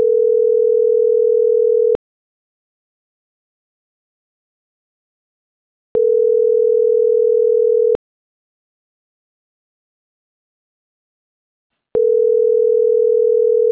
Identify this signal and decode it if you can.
Ringback tone (North America), 440 + 480 Hz.
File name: CENTRAL NAVIERAS / out onelinecapacitacion - .27552